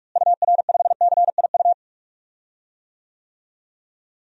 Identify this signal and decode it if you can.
Morse code, heard as a keyed tone, about 41 wpm, 700 Hz. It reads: UR5CIU
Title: DXCC-100 Call Signs - 6 Characters - Repeated 3x - 45wpm